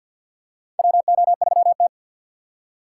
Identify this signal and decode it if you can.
Morse code, heard as a keyed tone, about 53 wpm, 700 Hz. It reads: WO3T
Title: USA Call Signs - 4 Characters - 3x Character Spacing - 50wpm